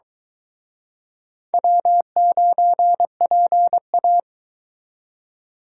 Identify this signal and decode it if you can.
Morse code, heard as a keyed tone, about 23 wpm, 700 Hz. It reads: W9PA